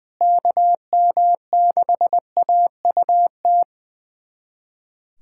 Morse code, heard as a keyed tone, about 20 wpm, 700 Hz. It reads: KM6AUT
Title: USA Callsign DE Callsign - 6 Characters - 20wpm